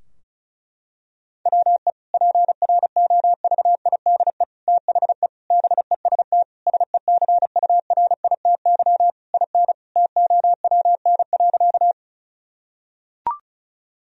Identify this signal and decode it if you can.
Morse code, heard as a keyed tone, about 36 wpm, 700 Hz. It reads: WE PROVIDE THE BEST SECURITY IN TOWN.
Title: Sentences from Top 1000 Words - Review Part 4 - 35wpm